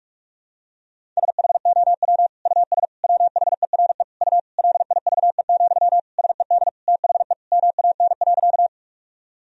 Morse code, heard as a keyed tone, about 46 wpm, 700 Hz. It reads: SHOW US WHERE U LIVE, SED THE MAN.
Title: Sentences from Top 200 Words and Abbreviations 45wpm